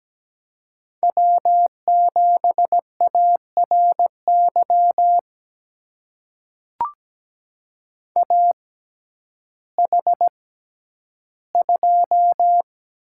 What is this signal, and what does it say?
Morse code, heard as a keyed tone, about 17 wpm, spaced as for 10 wpm, 700 Hz. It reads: W7ARY AH2
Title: USA Call Signs - 6x Character Spacing - 17wpm